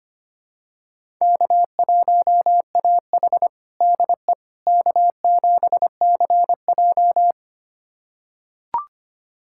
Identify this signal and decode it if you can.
Morse code, heard as a keyed tone, about 25 wpm, 700 Hz. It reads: K1AH DE K7CJ
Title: USA Callsign DE Callsign - 4 Characters - 25wpm